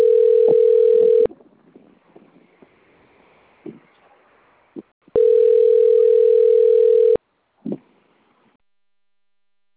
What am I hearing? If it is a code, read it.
Ringback tone (North America), 440 + 480 Hz.